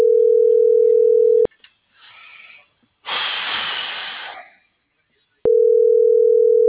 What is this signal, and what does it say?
Ringback tone (North America), 440 + 480 Hz.